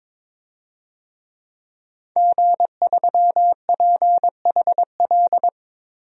Morse code, heard as a keyed tone, about 22 wpm, 700 Hz. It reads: G3PHL
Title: DXCC-100 Call Signs - 5 Characters - 22wpm